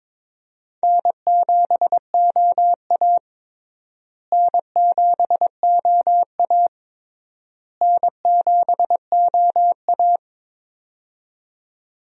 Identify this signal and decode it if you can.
Morse code, heard as a keyed tone, about 22 wpm, 700 Hz. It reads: N7OA N7OA N7OA
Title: USA Call Signs - 4 Characters - Repeated 3x - 22wpm